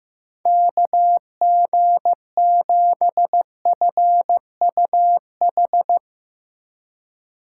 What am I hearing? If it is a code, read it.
Morse code, heard as a keyed tone, about 15 wpm, 700 Hz. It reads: KG7FUH
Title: USA Call Signs - 6 Characters - 2x Character Spacing - 15wpm